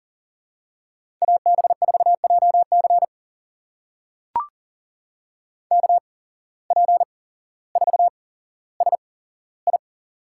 Morse code, heard as a keyed tone, about 39 wpm, spaced as for 17 wpm, 700 Hz. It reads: AB4JC KP4SI
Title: USA Call Signs - 5 Characters - 8x Character Spacing - 40wpm